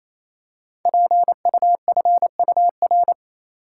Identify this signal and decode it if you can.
Morse code, heard as a keyed tone, about 27 wpm, 700 Hz. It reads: PUFUR
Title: Code Groups - 5 Letters - 28wpm